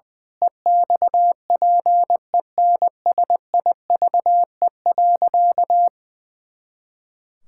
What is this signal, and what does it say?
Morse code, heard as a keyed tone, about 20 wpm, 700 Hz. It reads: EXPENSIVE.
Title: Pangram Challenge v2 - 20wpm